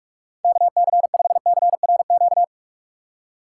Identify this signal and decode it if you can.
Morse code, heard as a keyed tone, about 44 wpm, 700 Hz. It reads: KC5CRQ
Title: USA Call Signs - 6 Characters - 6x Character Spacing - 45wpm